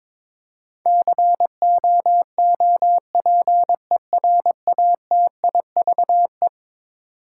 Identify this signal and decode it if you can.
Morse code, heard as a keyed tone, about 22 wpm, 700 Hz. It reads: COOPERATIVE
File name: practice / 22wpm / Eleven Letter Words - 22wpm